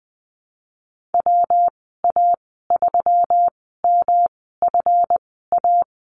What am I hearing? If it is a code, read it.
Morse code, heard as a keyed tone, about 20 wpm, spaced as for 15 wpm, 700 Hz. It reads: WA3MFA
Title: 173 Lesson 28g - Number 9 - Call Signs 20wpm